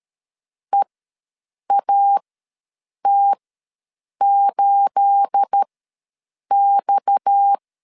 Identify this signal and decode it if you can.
Morse code, heard as a keyed tone, about 13 wpm, spaced as for 7 wpm, 800 Hz. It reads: EAT8X
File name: LEZIONI / 9 28 ws 12wpm 3354